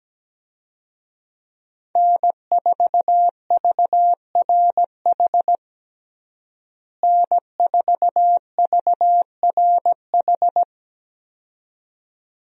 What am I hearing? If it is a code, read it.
Morse code, heard as a keyed tone, about 17 wpm, 700 Hz. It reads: N4VRH N4VRH
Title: USA Call Signs - 5 Characters - Repeated 2x - 17wpm